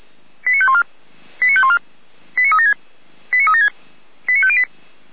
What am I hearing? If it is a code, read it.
Five-tone selective call (ZVEI), heard as quick runs of five tones, tones of about 70 ms. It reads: 89514 89514 89376 89376 89508